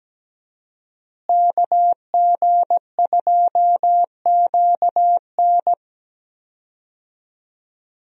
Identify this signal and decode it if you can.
Morse code, heard as a keyed tone, about 17 wpm, 700 Hz. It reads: KG2QN
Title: USA Call Signs - 5 Characters - Repeated 2x - 17wpm